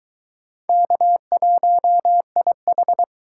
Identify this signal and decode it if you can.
Morse code, heard as a keyed tone, about 20 wpm, 700 Hz. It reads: K1IH